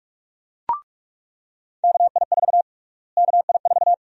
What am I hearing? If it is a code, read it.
Morse code, heard as a keyed tone, about 44 wpm, 700 Hz. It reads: KI4 KI4
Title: USA Call Sign Prefixes - Repeated 2x - 45wpm